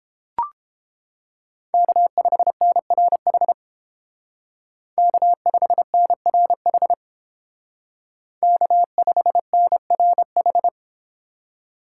Morse code, heard as a keyed tone, about 28 wpm, 700 Hz. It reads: K5NRH K5NRH K5NRH